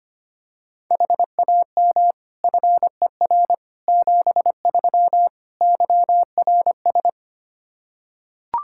Morse code, heard as a keyed tone, about 25 wpm, 700 Hz. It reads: HAM FER 73 YRS
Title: Practice QSO - Exchange 3 - More Info - 25wpm